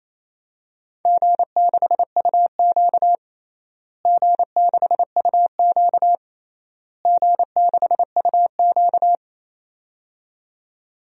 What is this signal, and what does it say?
Morse code, heard as a keyed tone, about 28 wpm, 700 Hz. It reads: G6UQ G6UQ G6UQ
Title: DXCC-100 Call Signs - 4 Characters - Repeated 3x - 28wpm